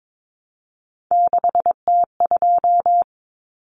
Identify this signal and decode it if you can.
Morse code, heard as a keyed tone, about 22 wpm, 700 Hz. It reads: 6T2
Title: Common Cut Numbers - 3-Digit 22wpm